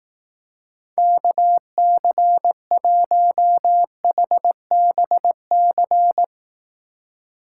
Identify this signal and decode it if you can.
Morse code, heard as a keyed tone, about 18 wpm, 700 Hz. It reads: KC1HBC
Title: Speed Racing - USA 2x3 Call Signs 15wpm